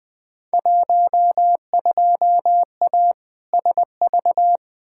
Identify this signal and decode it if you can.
Morse code, heard as a keyed tone, about 20 wpm, 700 Hz. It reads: 12A SV
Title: Field Day Exchanges Encore - 20wpm